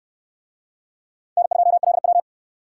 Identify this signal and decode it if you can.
Morse code, heard as a keyed tone, about 60 wpm, 700 Hz. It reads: N2LW